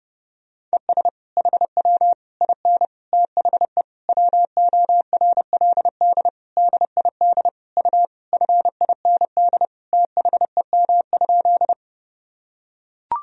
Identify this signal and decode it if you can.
Morse code, heard as a keyed tone, about 30 wpm, 700 Hz. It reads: ES HW IN THE WORLD DID U FIND THEM?
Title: Sentences from Top 200 Words and Abbreviations 30wpm